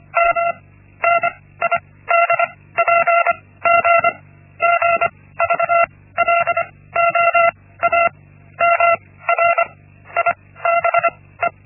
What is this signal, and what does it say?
Morse code, heard as a keyed tone, about 25 wpm, 680 Hz. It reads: M N I D P G G V L O A M R I B E
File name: Linksmakalnis-1